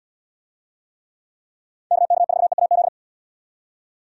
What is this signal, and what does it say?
Morse code, heard as a keyed tone, about 78 wpm, 700 Hz. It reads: KK4EAZ